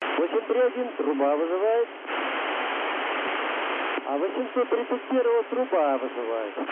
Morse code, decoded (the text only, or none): none